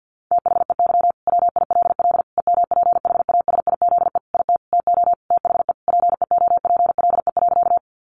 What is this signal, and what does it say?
Morse code, heard as a keyed tone, about 50 wpm, 700 Hz. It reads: THEY WILL EMPHASIZE IT TO THE PEOPLE.
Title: Sentences from Top 1500 Words 50wpm